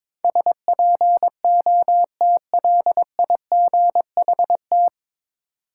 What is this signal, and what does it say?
Morse code, heard as a keyed tone, about 22 wpm, 700 Hz. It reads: SPOTLIGHT
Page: Morse Code Ninja practice 22wpm